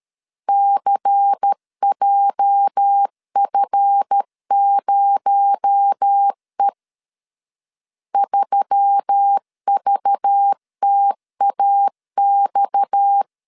Morse code, heard as a keyed tone, about 13 wpm, 800 Hz. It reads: CJF0E 3VTAX
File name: CW / DET / 1 9 1500C S C 12wpm 3811